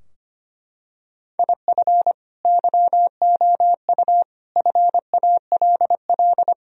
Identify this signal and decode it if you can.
Morse code, heard as a keyed tone, about 25 wpm, 700 Hz. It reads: IF YOU FALL
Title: Sentences from Top 1000 Words - Review Part 2 - 25wpm